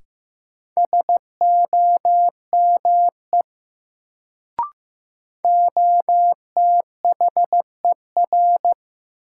Morse code, heard as a keyed tone, about 15 wpm, 700 Hz. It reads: SOME OTHER
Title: Sentences from Top 100 Words - New Word Drill - 15wpm